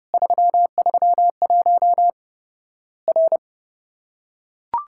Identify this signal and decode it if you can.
Morse code, heard as a keyed tone, about 30 wpm, 700 Hz. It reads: 331 R
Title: Worked All Europe DX Contest - QTC Exchange - 30wpm